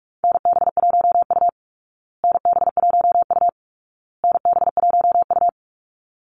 Morse code, heard as a keyed tone, about 44 wpm, 700 Hz. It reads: NB1U NB1U NB1U
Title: Call Signs Repeated 2x - 45wpm